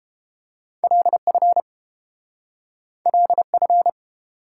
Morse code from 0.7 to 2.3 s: LF

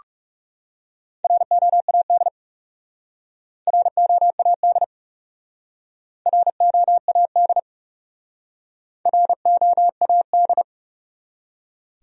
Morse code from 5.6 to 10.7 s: ROAD ROAD

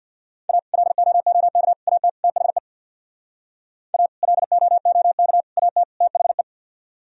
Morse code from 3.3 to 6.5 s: A LOOK AT THE

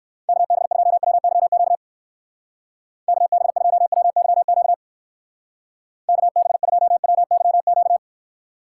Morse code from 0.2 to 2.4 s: KB2WYQ

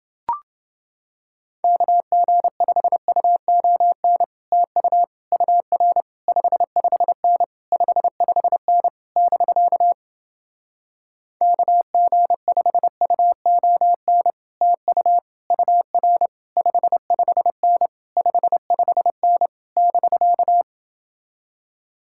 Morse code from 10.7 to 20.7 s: KG5UON TU UR 55N 55N <BK>

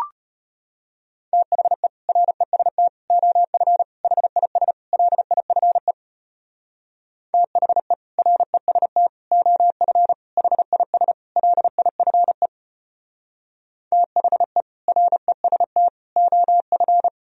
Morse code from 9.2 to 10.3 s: OF

